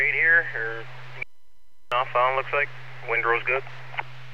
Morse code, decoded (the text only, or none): none